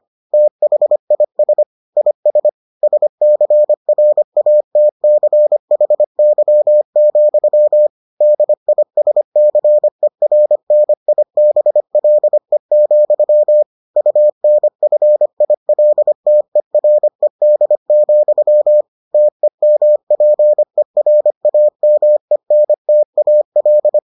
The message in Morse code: THIS IS SCRATCHY, DISCERNIBLE, UNFILTERED, TEMPERAMENTAL

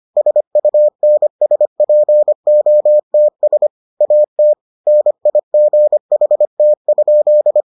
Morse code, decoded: SUNSPOTS AT NIGHT?

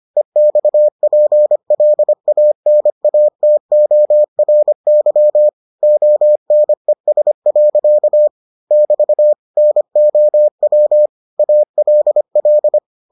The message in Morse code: EXPLANATORY ONES. <BT> NOW ALL

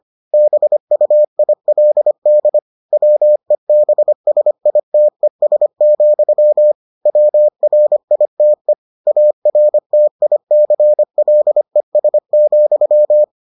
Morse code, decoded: BUILD WEBSITES, WRITE ARTICLES,

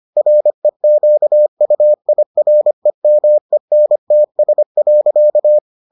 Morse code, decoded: REQUIREMENTS.

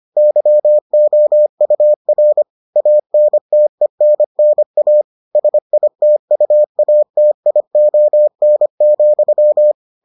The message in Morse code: YOUR ANTENNA SITUATION,

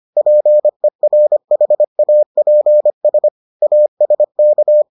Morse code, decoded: PERHAPS ASK

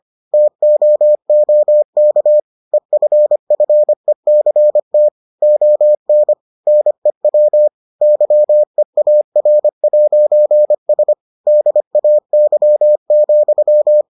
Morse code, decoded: TOOK EFFECT ON NEW YEAR'S DAY,

